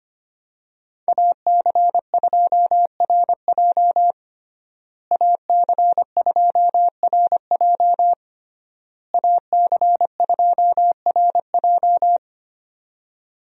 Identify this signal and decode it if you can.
Morse code, heard as a keyed tone, about 25 wpm, 700 Hz. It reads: AC2RJ AC2RJ AC2RJ
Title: USA Call Signs - 5 Characters - Repeated 3x - 25wpm